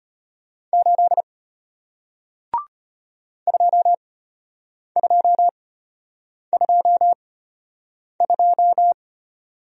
Morse code, heard as a keyed tone, about 34 wpm, spaced as for 8 wpm, 700 Hz. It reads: 8 2222